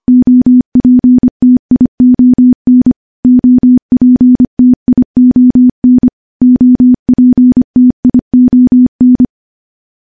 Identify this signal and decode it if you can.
Morse code, heard as a keyed tone, about 25 wpm, 250 Hz. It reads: OPTION OPTION OPTION
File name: morse / morse-training-25WPM